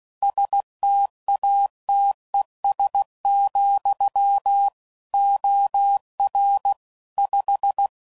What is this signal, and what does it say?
Morse code, heard as a keyed tone, about 16 wpm, 800 Hz. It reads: STATES, OR 5
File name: CW Code Groups @15WPM / Technical Subjects / AND Gate